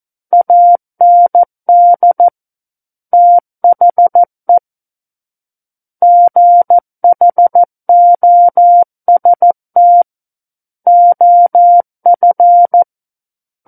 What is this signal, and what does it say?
Morse code, heard as a keyed tone, about 14 wpm, 700 Hz. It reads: AND THE GHOST OF
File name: Book One / 05-Chapter 5 - 14 WPM